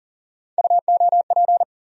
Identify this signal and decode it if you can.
Morse code, heard as a keyed tone, about 40 wpm, 700 Hz. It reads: UOP
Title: USA Call Sign Postfixes - 3 Letters 40wpm